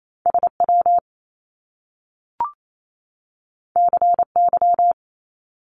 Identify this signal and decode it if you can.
Morse code, heard as a keyed tone, about 28 wpm, 700 Hz. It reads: SW CY
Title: Top Two Letter Combinations in Words 28wpm